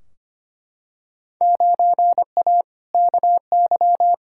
Morse code, heard as a keyed tone, about 25 wpm, 700 Hz. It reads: 9A KY
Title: Field Day Exchanges Encore - 25wpm